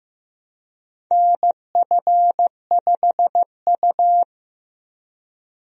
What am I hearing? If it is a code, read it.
Morse code, heard as a keyed tone, about 16 wpm, 700 Hz. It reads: NF5U
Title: USA Call Signs - 4 Characters - Repeated 2x - 15wpm